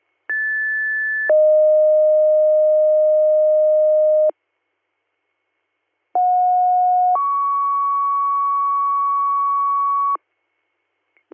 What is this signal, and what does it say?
Two-tone page, heard as a one-second tone, then a three-second tone, about 1750 then 620 Hz, about 730 then 1120 Hz. It is heard from 0.3 to 10.2 s.